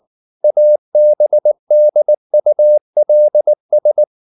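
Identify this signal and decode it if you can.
Morse code, heard as a keyed tone, about 19 wpm, 600 Hz. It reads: ABDULS
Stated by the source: Morse code at 19 wpm from CW med Gnister, reading Ken Follett's novel Never in English.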